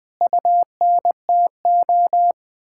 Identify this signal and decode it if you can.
Morse code, heard as a keyed tone, about 20 wpm, 700 Hz. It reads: UNTO
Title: Sentences with 8-Word Phrases - 20wpm